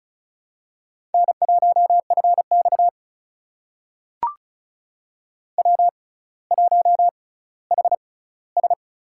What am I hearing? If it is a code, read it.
Morse code, heard as a keyed tone, about 36 wpm, spaced as for 18 wpm, 700 Hz. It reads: N1FX W1HS
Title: USA Call Signs - 4 Characters - 6x Character Spacing - 35wpm